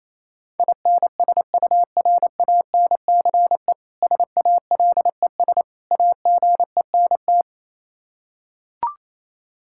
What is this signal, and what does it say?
Morse code, heard as a keyed tone, about 28 wpm, 700 Hz. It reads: INSURANCE SALES AGENT
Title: Practice QSO - Exchange 3 - More Info - 28wpm